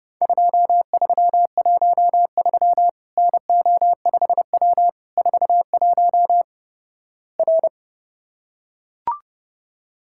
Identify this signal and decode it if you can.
Morse code, heard as a keyed tone, about 31 wpm, 700 Hz. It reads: 2313 NO5W 41 R
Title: Worked All Europe DX Contest - QTC Exchange - 30wpm